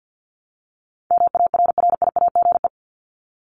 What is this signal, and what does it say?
Morse code, heard as a keyed tone, about 50 wpm, 700 Hz. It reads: MARRIAGE
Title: Top 751-1000 Words 50wpm